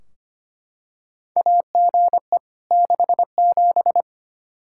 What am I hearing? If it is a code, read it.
Morse code, heard as a keyed tone, about 25 wpm, 700 Hz. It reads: AGE 67
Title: K1USN SST QSO Practice 25wpm